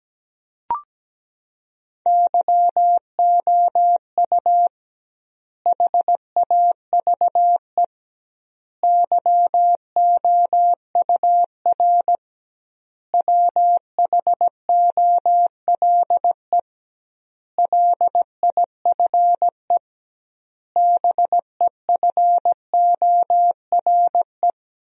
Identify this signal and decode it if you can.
Morse code, heard as a keyed tone, about 17 wpm, 700 Hz. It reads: YOU HAVE YOUR WHOLE LIFE BEFORE
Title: 2x Word-Spacing - Sentences from Top 400 Words - 17wpm